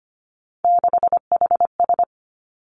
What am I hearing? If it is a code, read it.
Morse code, heard as a keyed tone, about 25 wpm, 700 Hz. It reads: 6HS
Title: Three Challenging Letter-Number Combinations 25wpm